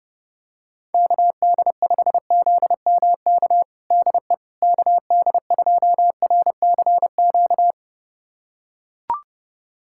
Morse code, heard as a keyed tone, about 30 wpm, 700 Hz. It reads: KD5ZMK DE KD2RCQ
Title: USA Callsign DE Callsign - 6 Characters - 30wpm